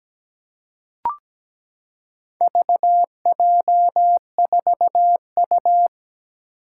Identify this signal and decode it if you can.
Morse code, heard as a keyed tone, about 17 wpm, 700 Hz. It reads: VJ4U
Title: Four Challenging Letter-Number Combinations 17wpm